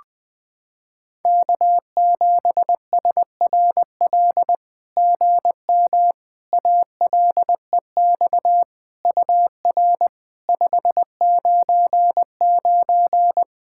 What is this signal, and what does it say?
Morse code, heard as a keyed tone, about 20 wpm, 700 Hz. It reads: K7SRL GM ALEX UR 599
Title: SOTA QSO Practice 20wpm